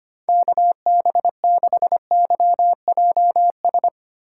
Morse code, heard as a keyed tone, about 25 wpm, 700 Hz. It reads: KB6YJS
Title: USA Call Signs - 2x Character Spacing - 25wpm